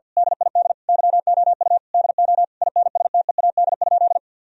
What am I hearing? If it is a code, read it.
Morse code, heard as a keyed tone, about 50 wpm, 700 Hz. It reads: DID YOU DO INSTEAD?
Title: Sentences from Top 1000 Words - Review Part 4 - 50wpm